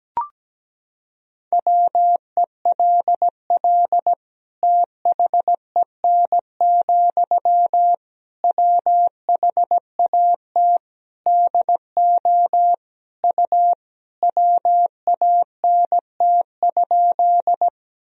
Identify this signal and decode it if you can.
Morse code, heard as a keyed tone, about 17 wpm, 700 Hz. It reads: WELL THEN, WHAT DO U WANT?
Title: Sentences from Top 200 Words and Abbreviations 17wpm